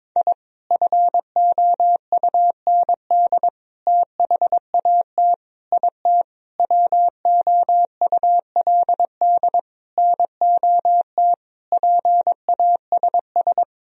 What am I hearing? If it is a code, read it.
Morse code, heard as a keyed tone, about 22 wpm, 700 Hz. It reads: I FOUND THAT IT WOULD NOT PASS
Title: Sentences from Top 1000 Words - Review Part 1 - 22wpm